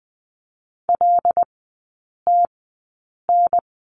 Morse code, heard as a keyed tone, about 20 wpm, spaced as for 8 wpm, 700 Hz. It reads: LTN
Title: 093 Lesson 16e - Letter c - Three Characters 20wpm